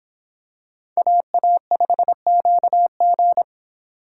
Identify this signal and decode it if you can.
Morse code, heard as a keyed tone, about 26 wpm, 700 Hz. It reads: AA5QG